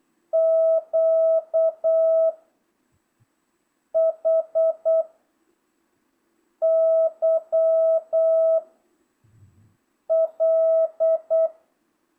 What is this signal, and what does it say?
Morse code, heard as a keyed tone, about 8 wpm, 660 Hz. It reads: Q H Y L